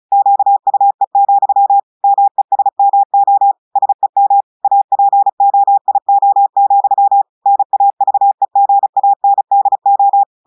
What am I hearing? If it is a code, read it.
Morse code, heard as a keyed tone, about 35 wpm, 800 Hz. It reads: QUE, MESMO SEM APOIO, NAVEGANDO
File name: audios / portugais / portugais texte en clair 1 vitesse 35 mots minutes